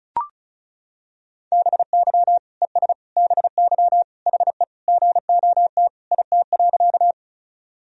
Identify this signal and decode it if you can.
Morse code, heard as a keyed tone, about 36 wpm, 700 Hz. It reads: BY ES BY HE GOT IT.
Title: Sentences from Top 300 Words and Abbreviations 35wpm